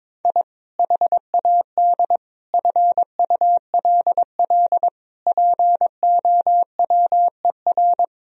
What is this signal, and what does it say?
Morse code, heard as a keyed tone, about 22 wpm, 700 Hz. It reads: I HAD FULL POWER
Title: Sentences from Top 1000 Words - Review Part 3 - 22wpm